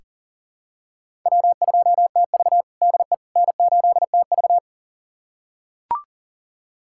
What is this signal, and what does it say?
Morse code, heard as a keyed tone, about 40 wpm, 700 Hz. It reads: W2TV DE N8TV
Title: USA Callsign DE Callsign - 4 Characters - 40wpm